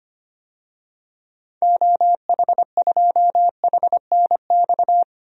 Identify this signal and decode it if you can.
Morse code, heard as a keyed tone, about 25 wpm, 700 Hz. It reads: OH2HNX
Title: DXCC-100 Call Signs - 6 Characters - 25wpm